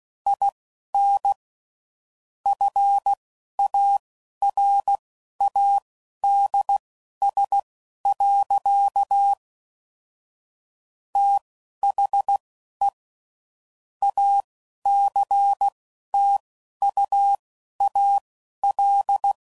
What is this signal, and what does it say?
Morse code, heard as a keyed tone, about 16 wpm, spaced as for 11 wpm, 800 Hz. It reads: IN FARADS. THE ACTUAL
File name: CW Code Groups @10WPM / Technical Subjects / XC